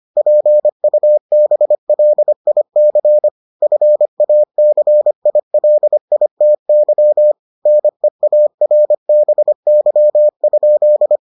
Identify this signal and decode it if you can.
Morse code, heard as a keyed tone, about 25 wpm, 600 Hz. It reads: PUBLIC FACILITY NEARBY?